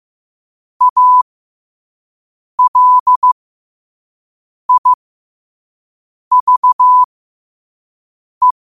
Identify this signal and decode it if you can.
Morse code, heard as a keyed tone, about 15 wpm, spaced as for 5 wpm, 1000 Hz. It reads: ALIVE